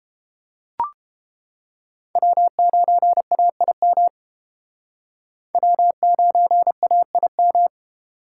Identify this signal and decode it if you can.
Morse code, heard as a keyed tone, about 30 wpm, 700 Hz. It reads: W9AIM W9AIM